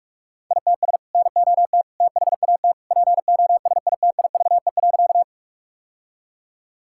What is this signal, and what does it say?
Morse code, heard as a keyed tone, about 42 wpm, 700 Hz. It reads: ITS NOT THAT POSITIVE.
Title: Sentences from Top 1000 Words - Review Part 5 - 45wpm